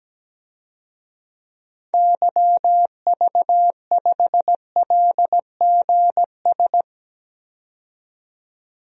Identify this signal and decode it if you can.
Morse code, heard as a keyed tone, about 17 wpm, 700 Hz. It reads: YV5LGS
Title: DXCC-100 Call Signs - 6 Characters - 17wpm